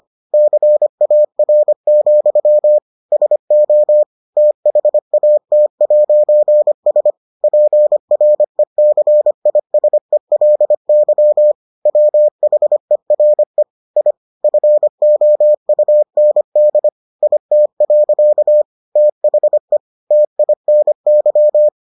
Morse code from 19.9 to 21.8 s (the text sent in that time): TINY